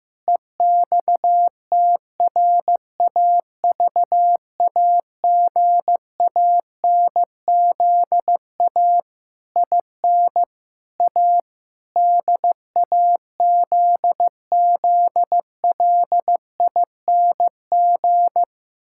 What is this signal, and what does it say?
Morse code, heard as a keyed tone, about 15 wpm, 700 Hz. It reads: EXTRAVAGANZA IN A DAZZLING
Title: Pangram Challenge v2 - 15wpm